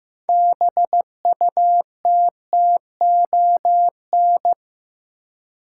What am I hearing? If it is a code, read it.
Morse code, heard as a keyed tone, about 15 wpm, 700 Hz. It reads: BUTTON
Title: Six Letter Words - 15wpm